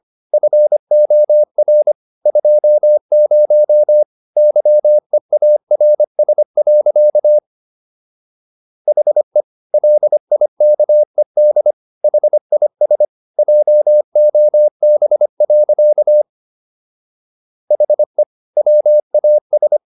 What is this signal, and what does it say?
Morse code, heard as a keyed tone, about 25 wpm, 600 Hz. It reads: FOR 20 YEARS. HE LIKED HIS JOB. HE WAS